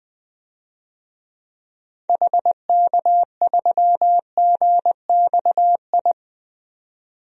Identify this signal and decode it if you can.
Morse code, heard as a keyed tone, about 20 wpm, 700 Hz. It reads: HK3GXI
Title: DXCC-100 Call Signs - 6 Characters - 20wpm